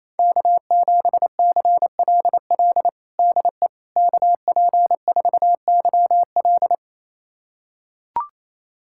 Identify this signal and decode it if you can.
Morse code, heard as a keyed tone, about 28 wpm, 700 Hz. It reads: K7CLL DE KP4YL
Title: USA Callsign DE Callsign - 5 Characters - 28wpm